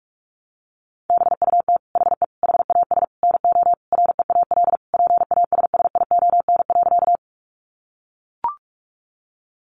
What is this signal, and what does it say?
Morse code, heard as a keyed tone, about 44 wpm, 700 Hz. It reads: BUT HE HAS NO REAL PASSION.
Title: Sentences from Top 1300 Words 45wpm